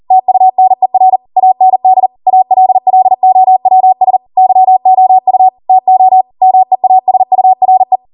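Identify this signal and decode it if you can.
Morse code, heard as a keyed tone, about 40 wpm, 750 Hz. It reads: TUNER AND ALLOWS YOU TO MEASURE
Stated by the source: Code practice at 40 wpm.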